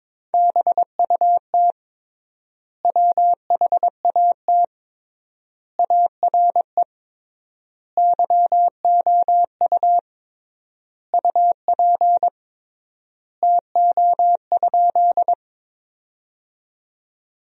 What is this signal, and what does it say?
Morse code, heard as a keyed tone, about 22 wpm, 700 Hz. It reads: BUT WHAT ARE YOU UP TO?
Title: 3x Word-Spacing - Sentences from Top 100 Words - 22wpm